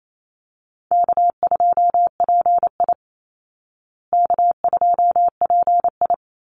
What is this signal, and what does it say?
Morse code, heard as a keyed tone, about 28 wpm, 700 Hz. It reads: K2PI K2PI
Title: Easy Call Signs Repeated 3x - 28wpm